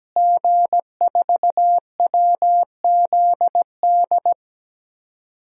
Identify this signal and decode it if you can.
Morse code, heard as a keyed tone, about 17 wpm, 700 Hz. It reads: G4WZD
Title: DXCC-100 Call Signs - 5 Characters - Repeated 2x - 17wpm